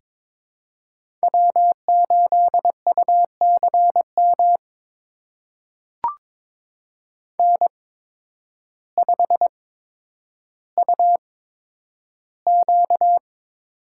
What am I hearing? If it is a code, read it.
Morse code, heard as a keyed tone, about 22 wpm, spaced as for 10 wpm, 700 Hz. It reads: W8UCM N5UQ
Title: USA Call Signs - 5 Characters - 8x Character Spacing - 22wpm